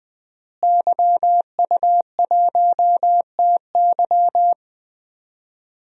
Morse code, heard as a keyed tone, about 20 wpm, 700 Hz. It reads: YU1TY
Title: DXCC-100 Call Signs - 5 Characters - Repeated 2x - 20wpm